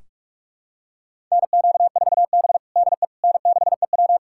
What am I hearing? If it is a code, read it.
Morse code, heard as a keyed tone, about 41 wpm, 700 Hz. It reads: NQ4B DE N6EW
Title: USA Callsign DE Callsign - 4 Characters - 45wpm